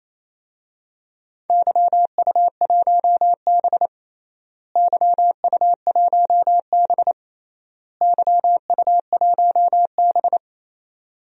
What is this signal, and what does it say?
Morse code, heard as a keyed tone, about 28 wpm, 700 Hz. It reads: YU1B YU1B YU1B
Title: DXCC-100 Call Signs - 4 Characters - Repeated 3x - 28wpm